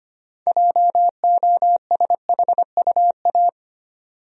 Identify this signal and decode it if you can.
Morse code, heard as a keyed tone, about 25 wpm, 700 Hz. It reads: JOSHUA
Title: USA Common Names - 25wpm